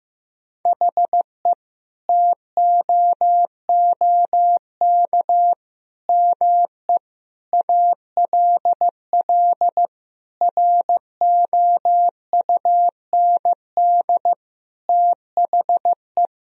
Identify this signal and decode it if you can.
Morse code, heard as a keyed tone, about 15 wpm, 700 Hz. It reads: HE TOOK ME ALL ROUND THE
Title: Sentences from Top 1000 Words - Review Part 1 - 15wpm